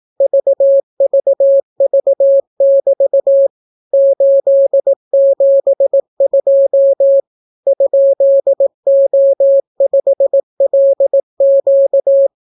Morse code, heard as a keyed tone, about 18 wpm, 550 Hz. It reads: VVV<BT> 872 ?O5LQ